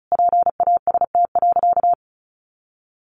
Morse code, heard as a keyed tone, about 34 wpm, 700 Hz. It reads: PAST.
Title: Sentences from Top 1300 Words 35wpm